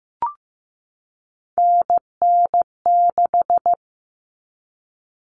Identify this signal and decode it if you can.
Morse code, heard as a keyed tone, about 15 wpm, 700 Hz. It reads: NN6
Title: Common Cut Numbers - 3-Digit 15wpm